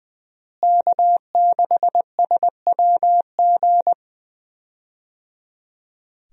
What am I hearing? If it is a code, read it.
Morse code, heard as a keyed tone, about 20 wpm, 700 Hz. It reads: K6SWG
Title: USA Call Signs - 5 Characters - Repeated 3x - 20wpm